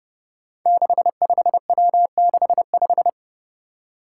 Morse code, heard as a keyed tone, about 30 wpm, 700 Hz. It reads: 65W65
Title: Five Challenging Letter-Number Combinations 30wpm